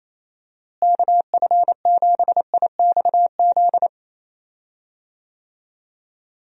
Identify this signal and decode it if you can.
Morse code, heard as a keyed tone, about 28 wpm, 700 Hz. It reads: KF7IXZ
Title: USA Call Signs - 6 Characters - Repeated 2x - 28wpm